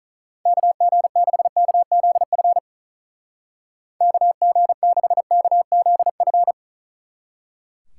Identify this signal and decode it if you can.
Morse code, heard as a keyed tone, about 44 wpm, 700 Hz. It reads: KG6KZF KG6KZF